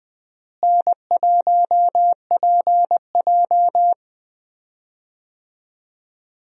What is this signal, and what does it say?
Morse code, heard as a keyed tone, about 20 wpm, 700 Hz. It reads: N1PJ
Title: USA Call Signs - 4 Characters - Repeated 2x - 20wpm